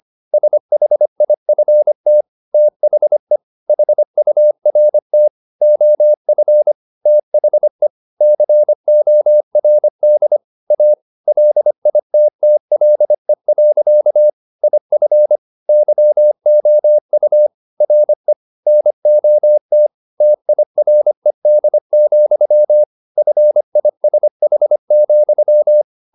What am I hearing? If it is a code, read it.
Morse code, heard as a keyed tone, about 25 wpm, 600 Hz. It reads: SHIFT THE HURT OF THE CORD A LITTLE. IF YOU RE NOT TIRED, FISH,
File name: the old man and the sea part 1 25wpm